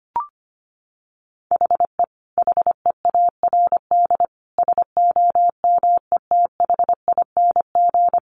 Morse code, heard as a keyed tone, about 25 wpm, 700 Hz. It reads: HE HEARD SOMETHING
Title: Sentences from Top 800 Words 25wpm